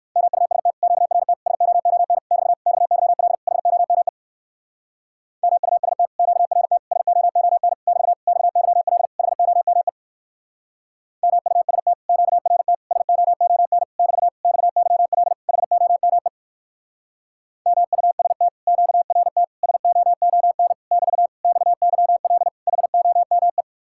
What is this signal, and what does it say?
Morse code, heard as a keyed tone, about 59 wpm, 700 Hz. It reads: MUST QRT SOON <BT> XYL HOME MUST QRT SOON <BT> XYL HOME MUST QRT SOON <BT> XYL HOME MUST QRT SOON <BT> XYL HOME